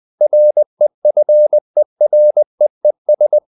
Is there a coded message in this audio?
Yes: Morse code from 0.2 to 3.4 s, heard as a keyed tone, about 20 wpm, 600 Hz. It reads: REFEREES